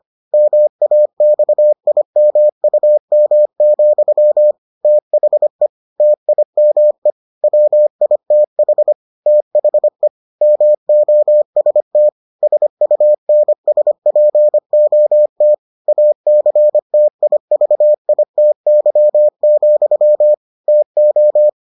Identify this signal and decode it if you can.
Morse code, heard as a keyed tone, about 25 wpm, 600 Hz. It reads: MAXIMUM, THE TIME WITH THE MOST SUNSPOT ACTIVITY, TO